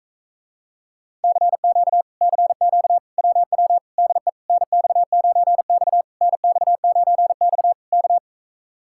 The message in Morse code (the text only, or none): CQ CQ WW DE NX9X NX9X K